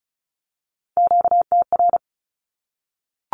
Morse code, heard as a keyed tone, about 34 wpm, 700 Hz. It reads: QTR
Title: Q-Codes Repeated 3x 35wpm